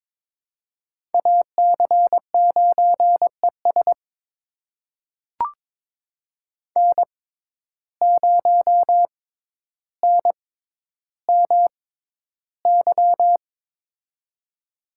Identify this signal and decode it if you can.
Morse code, heard as a keyed tone, about 22 wpm, spaced as for 11 wpm, 700 Hz. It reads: AC9ES N0NMY
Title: USA Call Signs - 5 Characters - 6x Character Spacing - 22wpm